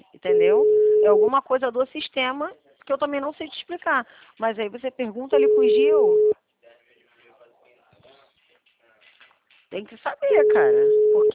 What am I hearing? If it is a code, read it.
Ringback tone (Europe, 425 Hz), with speech.
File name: out- -2039- - - .72706